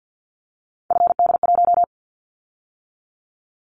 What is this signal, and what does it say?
Morse code, heard as a keyed tone, about 50 wpm, 700 Hz. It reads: FD1
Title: Callsign Prefixes for Top 50 DXCC - 50wpm